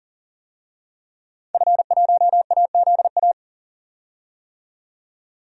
Morse code, heard as a keyed tone, about 40 wpm, 700 Hz. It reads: F1AZA
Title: DXCC-100 Call Signs - 5 Characters - 40wpm